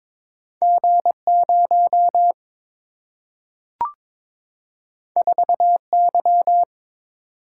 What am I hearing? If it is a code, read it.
Morse code, heard as a keyed tone, about 22 wpm, 700 Hz. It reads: G 0 4 Y
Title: Single-Letter Single-Number 22wpm